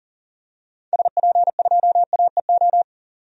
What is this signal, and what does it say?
Morse code, heard as a keyed tone, about 40 wpm, 700 Hz. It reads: SP2AEO
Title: DXCC-100 Call Signs - 6 Characters - 40wpm